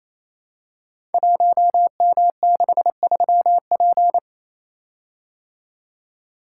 Morse code, heard as a keyed tone, about 27 wpm, 700 Hz. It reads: 1M63P
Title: Code Groups - 5 Letter:Numbers - 28wpm